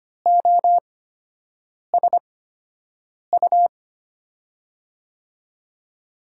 Morse code, heard as a keyed tone, about 25 wpm, spaced as for 7 wpm, 700 Hz. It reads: OSU
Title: USA Call Signs - 5 Characters - 8x Character Spacing - 25wpm